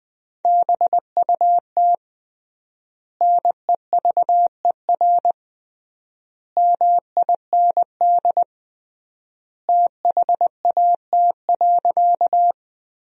Morse code, heard as a keyed tone, about 20 wpm, 700 Hz. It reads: BUT NEVER MIND THAT.
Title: 3x Word-Spacing - Sentences from Top 500 Words - 20wpm